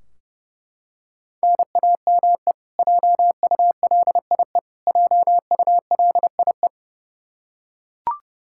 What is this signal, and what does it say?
Morse code, heard as a keyed tone, about 30 wpm, 700 Hz. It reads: NAME JULIE JULIE
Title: Practice QSO - Exchange 1 - Beginning - 30wpm